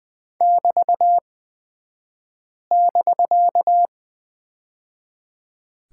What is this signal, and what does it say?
Morse code, heard as a keyed tone, about 20 wpm, spaced as for 5 wpm, 700 Hz. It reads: <BT><BK>
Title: 249 Lesson 40d - BK - Two Characters 20wpm